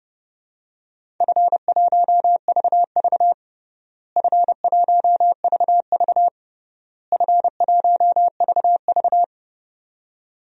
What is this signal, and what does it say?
Morse code, heard as a keyed tone, about 30 wpm, 700 Hz. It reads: F1VV F1VV F1VV
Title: DXCC-100 Call Signs - 4 Characters - Repeated 3x - 30wpm